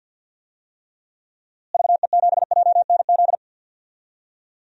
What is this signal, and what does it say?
Morse code, heard as a keyed tone, about 53 wpm, 700 Hz. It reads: VE7JNZ
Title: DXCC-100 Call Signs - 6 Characters - Repeated 2x - 50wpm